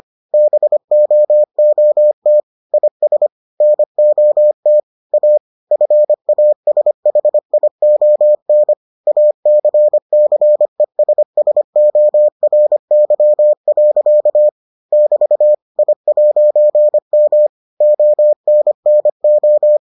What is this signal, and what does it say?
Morse code, heard as a keyed tone, about 25 wpm, 600 Hz. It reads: BOOT IS NOT A FASHION ACCESSORY. <BT> I'M ONNO